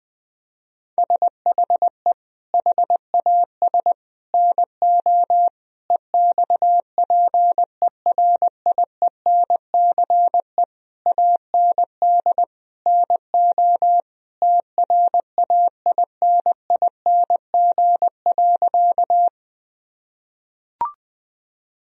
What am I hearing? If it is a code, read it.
Morse code, heard as a keyed tone, about 21 wpm, 700 Hz. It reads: SHE HAS NO EXPERIENCE AND NO TRAINING.
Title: Sentences from Top 1000 Words - Review Part 4 - 20wpm